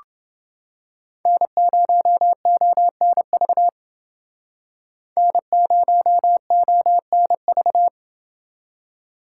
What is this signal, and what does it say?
Morse code, heard as a keyed tone, about 29 wpm, 700 Hz. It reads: N0ONV N0ONV